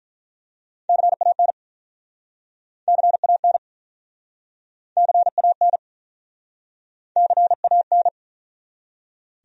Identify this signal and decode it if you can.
Morse code, heard as a keyed tone, about 45 wpm, 700 Hz. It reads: CAN CAN CAN CAN